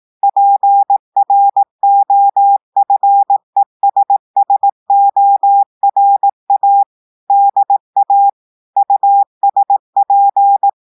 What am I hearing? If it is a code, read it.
Morse code, heard as a keyed tone, about 18 wpm, 800 Hz. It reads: PROFESSORA DA USP
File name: audios / portugais / portugais texte en clair 1 vitesse 18 mots minutes